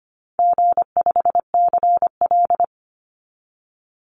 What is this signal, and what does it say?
Morse code, heard as a keyed tone, about 25 wpm, 700 Hz. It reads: G5CL
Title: Easy Call Signs Repeated 2x - 25wpm